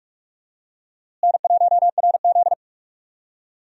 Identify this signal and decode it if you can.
Morse code, heard as a keyed tone, about 42 wpm, 700 Hz. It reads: N1RZ
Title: USA Call Signs - 4 Characters - Repeated 3x - 45wpm